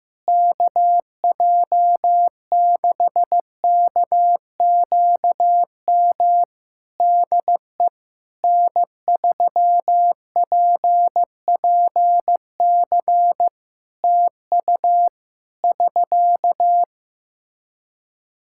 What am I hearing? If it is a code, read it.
Morse code, heard as a keyed tone, about 15 wpm, 700 Hz. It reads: KJ6KQM DE N3PPC TU <SK>
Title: K1USN SST QSO Practice 15wpm